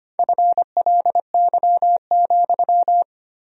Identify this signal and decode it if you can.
Morse code, heard as a keyed tone, about 25 wpm, 700 Hz. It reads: FLY,
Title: 2x Word-Spacing - Sentences from Top 500 Words - 25wpm